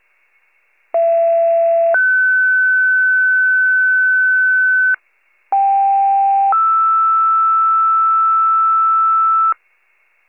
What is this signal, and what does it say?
Two-tone page, heard as a one-second tone, then a three-second tone, about 670 then 1500 Hz, about 790 then 1320 Hz. It is heard from 0.9 to 9.5 s.